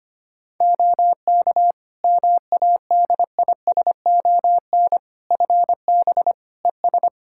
Morse code, heard as a keyed tone, about 25 wpm, 700 Hz. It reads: OK MADISON FB ES
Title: Practice QSO - Exchange 2 - Info - 25wpm